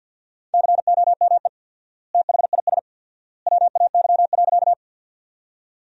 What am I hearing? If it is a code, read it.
Morse code, heard as a keyed tone, about 50 wpm, 700 Hz. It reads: COME THIS WAY.
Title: 4x Word-Spacing - Sentences from Top 100 Words - 50wpm